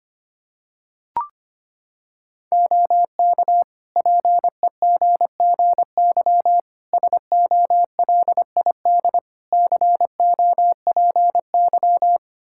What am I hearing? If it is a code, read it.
Morse code, heard as a keyed tone, about 25 wpm, 700 Hz. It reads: OK PEGGY SOLID COPY
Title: Practice QSO - Exchange 3 - More Info - 25wpm